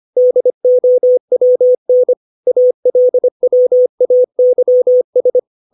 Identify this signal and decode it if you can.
Morse code, heard as a keyed tone, about 25 wpm, 500 Hz. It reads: DOWN ALWAYS
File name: 632 words / 632 Words 25WPM